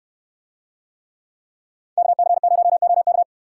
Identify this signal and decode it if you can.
Morse code, heard as a keyed tone, about 63 wpm, 700 Hz. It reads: KC0OK